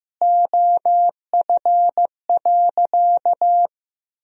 Morse code, heard as a keyed tone, about 15 wpm, 700 Hz. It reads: OF.
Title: 3x Word-Spacing - Sentences from Top 300 Words - 15wpm